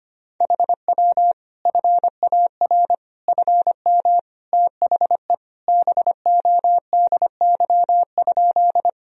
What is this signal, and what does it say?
Morse code, heard as a keyed tone, about 25 wpm, 700 Hz. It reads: HW FAR FM THE BODY?
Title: Sentences from Top 400 Words and Abbreviations 25wpm